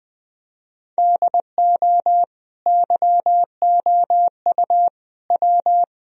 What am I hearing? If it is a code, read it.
Morse code, heard as a keyed tone, about 21 wpm, 700 Hz. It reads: DO YOU W
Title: Sentences from Top 1000 Words - Review Part 2 - 20wpm